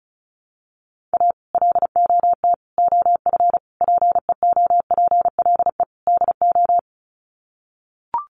Morse code, heard as a keyed tone, about 34 wpm, 700 Hz. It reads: A LOT OF PEOPLE DO
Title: Sets of 5 Words 35wpm